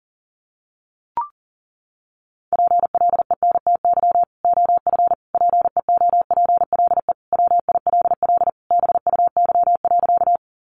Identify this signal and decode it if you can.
Morse code, heard as a keyed tone, about 37 wpm, 700 Hz. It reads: PLENTY OF PEOPLE WILL BUY.
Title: Sentences from Top 1300 Words 40wpm